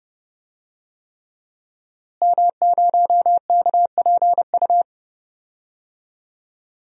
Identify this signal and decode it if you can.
Morse code, heard as a keyed tone, about 30 wpm, 700 Hz. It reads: M0KPU
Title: DXCC-100 Call Signs - 5 Characters - Repeated 3x - 30wpm